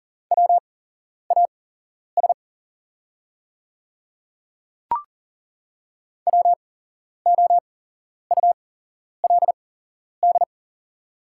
Morse code, heard as a keyed tone, about 40 wpm, spaced as for 11 wpm, 690 Hz. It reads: WAS WOULD
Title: 8x Character Spacing - Top 100 Words - 40wpm